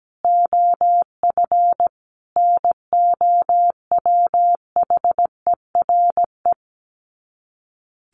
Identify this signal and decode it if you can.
Morse code, heard as a keyed tone, about 17 wpm, 700 Hz. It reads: OF NOWHERE
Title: Sets of 5 Words Encore 17wpm